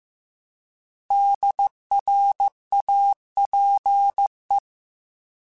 Morse code, heard as a keyed tone, about 15 wpm, 800 Hz. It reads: DRAPE